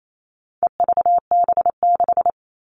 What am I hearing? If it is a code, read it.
Morse code, heard as a keyed tone, about 28 wpm, 700 Hz. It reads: EVB6
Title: All Cut Numbers - 4-Digit 28wpm